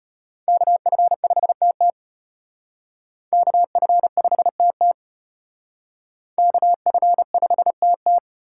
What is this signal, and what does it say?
Morse code, heard as a keyed tone, about 32 wpm, 700 Hz. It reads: KF5TT KF5TT KF5TT